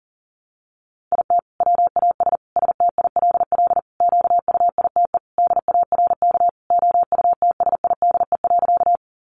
Morse code, heard as a keyed tone, about 37 wpm, 700 Hz. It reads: IT WAS STILL QUITE DARK OUTSIDE.
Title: Sentences from Top 900 Words 40wpm